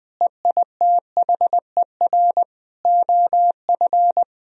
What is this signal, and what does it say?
Morse code, heard as a keyed tone, about 20 wpm, 700 Hz. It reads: EITHER OF
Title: Sentences from Top 1000 Words - Review Part 4 - 20wpm